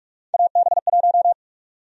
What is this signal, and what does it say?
Morse code, heard as a keyed tone, about 44 wpm, 700 Hz. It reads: AB1
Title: USA Call Sign Prefixes - 45wpm